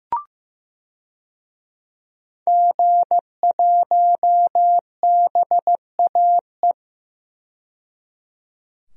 Morse code, heard as a keyed tone, about 15 wpm, 700 Hz. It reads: G1BAE
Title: DXCC-100 Call Signs - 5 Characters - 15wpm